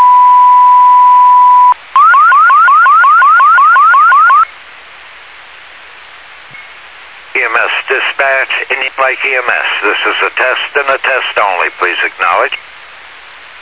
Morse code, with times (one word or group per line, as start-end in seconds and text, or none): none